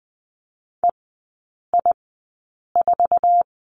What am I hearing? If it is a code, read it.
Morse code, heard as a keyed tone, about 20 wpm, spaced as for 8 wpm, 700 Hz. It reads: EI4
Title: 047 Lesson 9e - Number 4 - Three Characters 20wpm